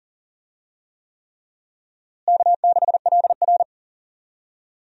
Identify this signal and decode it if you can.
Morse code, heard as a keyed tone, about 40 wpm, 700 Hz. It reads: K6LR
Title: USA Call Signs - 4 Characters - Repeated 3x - 40wpm